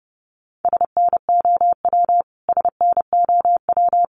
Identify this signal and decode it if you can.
Morse code, heard as a keyed tone, about 30 wpm, 700 Hz. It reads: SNOW SNOW
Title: Top 500 Words Repeated 30wpm